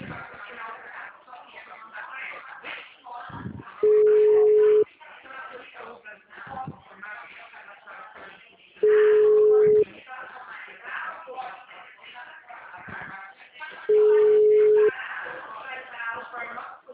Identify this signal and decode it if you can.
Ringback tone (Europe, 425 Hz), with speech.